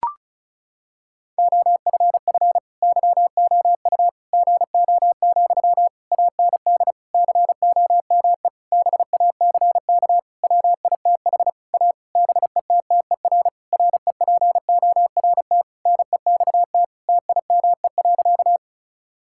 Morse code, heard as a keyed tone, about 36 wpm, 700 Hz. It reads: OFF YOU GO, AND COME BACK WITH A BETTER REPORT NEXT TIME.
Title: Sentences from Top 1000 Words - Review Part 6 - 35wpm